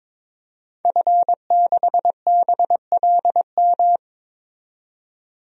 Morse code, heard as a keyed tone, about 22 wpm, 700 Hz. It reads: F6BLM
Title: DXCC-100 Call Signs - 5 Characters - 22wpm